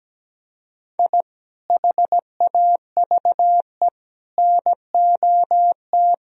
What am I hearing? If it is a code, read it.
Morse code, heard as a keyed tone, about 17 wpm, 700 Hz. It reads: I HAVE NOT
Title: Sentences from Top 1000 Words - Review Part 1 - 17wpm